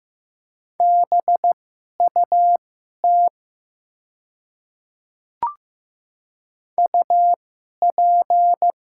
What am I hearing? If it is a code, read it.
Morse code, heard as a keyed tone, about 15 wpm, spaced as for 11 wpm, 700 Hz. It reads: BUT UP
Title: 2x Character Spacing - Top 100 Words - 15wpm